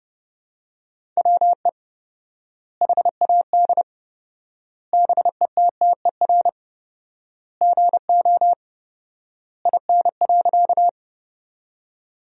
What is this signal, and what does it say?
Morse code, heard as a keyed tone, about 28 wpm, 700 Hz. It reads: WE HAD BETTER GO IN.
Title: 4x Word-Spacing - Sentences from Top 400 Words - 30wpm